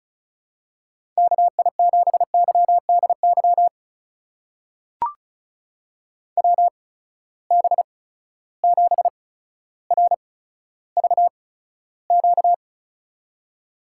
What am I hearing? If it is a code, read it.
Morse code, heard as a keyed tone, about 35 wpm, spaced as for 15 wpm, 700 Hz. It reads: KI7YDY WB7RVQ
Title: USA Call Signs - 6 Characters - 8x Character Spacing - 35wpm